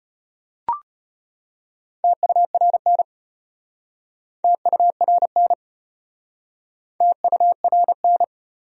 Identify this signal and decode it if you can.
Morse code, heard as a keyed tone, about 34 wpm, 700 Hz. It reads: TURN TURN TURN